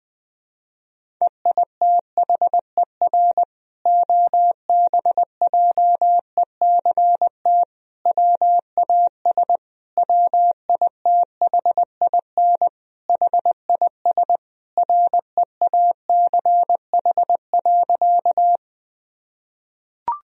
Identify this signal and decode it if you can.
Morse code, heard as a keyed tone, about 21 wpm, 700 Hz. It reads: EITHER OBJECT WAS WITHIN HIS REACH.
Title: Sentences from Top 1000 Words - Review Part 4 - 20wpm